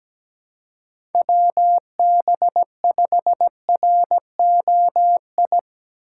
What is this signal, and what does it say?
Morse code, heard as a keyed tone, about 19 wpm, 700 Hz. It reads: WB5ROI
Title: Speed Racing - USA 2x3 Call Signs 17wpm